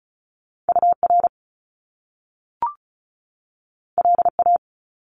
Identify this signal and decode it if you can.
Morse code, heard as a keyed tone, about 36 wpm, 700 Hz. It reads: U R L A
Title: Top Two Letter Combinations in Words 35wpm